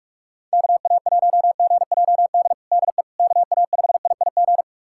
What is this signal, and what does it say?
Morse code, heard as a keyed tone, about 46 wpm, 700 Hz. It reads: KA1GJD DE KA5IIG
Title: USA Callsign DE Callsign - 6 Characters - 45wpm